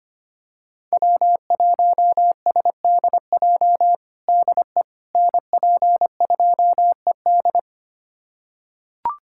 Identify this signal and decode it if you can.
Morse code, heard as a keyed tone, about 25 wpm, 700 Hz. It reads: W1SDJ DE NP2ED
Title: USA Callsign DE Callsign - 5 Characters - 25wpm